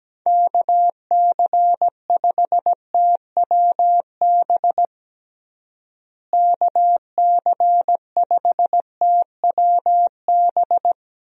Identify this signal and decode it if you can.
Morse code, heard as a keyed tone, about 17 wpm, 700 Hz. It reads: KC5TWB KC5TWB
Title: USA Call Signs - 6 Characters - Repeated 2x - 17wpm